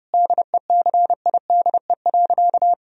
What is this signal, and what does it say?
Morse code, heard as a keyed tone, about 30 wpm, 700 Hz. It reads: DECIDE.
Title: Sentences from Top 1000 Words - Review Part 3 - 30wpm